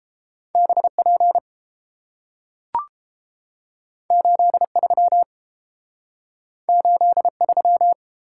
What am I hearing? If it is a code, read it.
Morse code, heard as a keyed tone, about 30 wpm, 700 Hz. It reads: BP 83 83